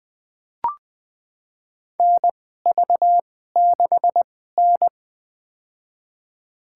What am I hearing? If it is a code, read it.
Morse code, heard as a keyed tone, about 20 wpm, spaced as for 15 wpm, 700 Hz. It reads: NV6N
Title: USA Call Signs - 4 Characters - 2x Character Spacing - 20wpm